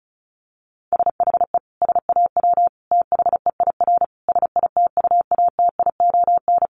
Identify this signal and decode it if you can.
Morse code, heard as a keyed tone, about 34 wpm, 700 Hz. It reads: SHE SAW THEIR SITUATION
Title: Sentences from Top 1200 Words 35wpm